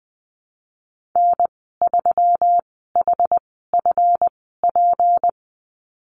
Morse code, heard as a keyed tone, about 20 wpm, spaced as for 14 wpm, 700 Hz. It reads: N3HFP
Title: 154 Lesson 25g - Letter p - Call Signs 20wpm